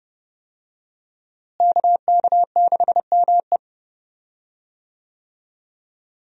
Morse code, heard as a keyed tone, about 31 wpm, 700 Hz. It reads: KK6ME
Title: USA Call Signs - 5 Characters - Repeated 2x - 30wpm